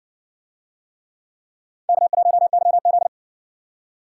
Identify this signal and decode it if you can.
Morse code, heard as a keyed tone, about 56 wpm, 700 Hz. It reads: K1YZ